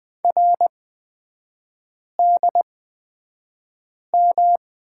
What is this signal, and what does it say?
Morse code, heard as a keyed tone, about 20 wpm, spaced as for 5 wpm, 700 Hz. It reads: RDM
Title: 224 Lesson 36e - BT - Three Characters 20wpm